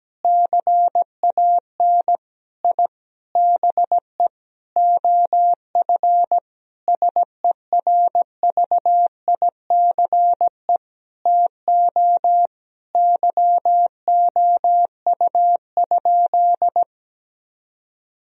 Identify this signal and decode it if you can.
Morse code, heard as a keyed tone, about 17 wpm, 700 Hz. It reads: CAN I BE OF SERVICE TO YOU?
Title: Sentences from Top 1000 Words - Review Part 4 - 17wpm